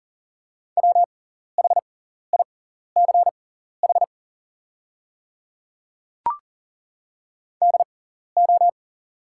Morse code, heard as a keyed tone, about 40 wpm, spaced as for 14 wpm, 690 Hz. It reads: WHICH DO
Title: 6x Character Spacing - Top 100 Words - 40wpm